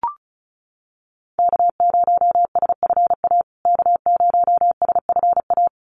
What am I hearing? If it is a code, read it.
Morse code, heard as a keyed tone, about 34 wpm, 700 Hz. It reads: K0SFA K0SFA
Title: USA Call Signs Repeated 2x 35wpm